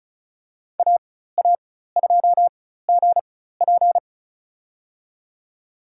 Morse code, heard as a keyed tone, about 34 wpm, spaced as for 16 wpm, 700 Hz. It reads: AA2GP
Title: USA Call Signs - 5 Characters - 4x Character Spacing - 35wpm